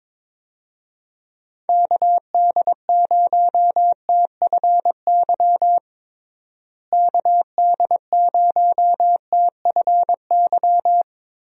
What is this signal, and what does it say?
Morse code, heard as a keyed tone, about 22 wpm, 700 Hz. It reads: KD0TFY KD0TFY
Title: USA Call Signs - 6 Characters - Repeated 2x - 22wpm